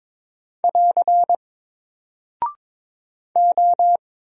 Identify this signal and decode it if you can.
Morse code, heard as a keyed tone, about 22 wpm, 700 Hz. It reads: <AR> O
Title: Single Character (Letter:Number:Punctuation:Prosign) 22wpm